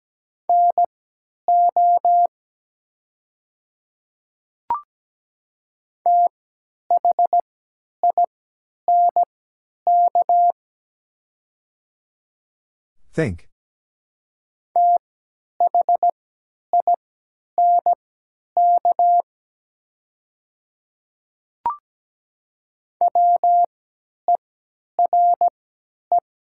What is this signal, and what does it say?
Morse code, heard as a keyed tone, about 17 wpm, spaced as for 10 wpm, 700 Hz. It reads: NO THINK THINK WERE
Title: 3x Character Spacing - Top 200 Words - 17wpm